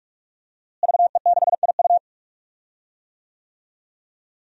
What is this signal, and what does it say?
Morse code, heard as a keyed tone, about 47 wpm, 700 Hz. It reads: VE6IU
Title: DXCC-100 Call Signs - 5 Characters - Repeated 3x - 45wpm